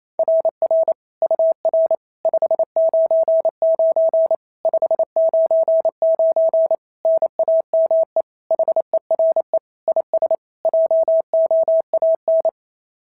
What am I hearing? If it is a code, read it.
Morse code, heard as a keyed tone, about 28 wpm, 690 Hz. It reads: RR UR 599 599 NAME HERE IS JOAN
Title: SOTA QSO Practice 28wpm